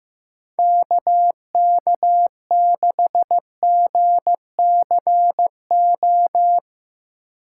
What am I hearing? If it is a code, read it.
Morse code, heard as a keyed tone, about 15 wpm, 700 Hz. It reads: KK6GCO
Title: USA Call Signs - 8x Character Spacing - 15wpm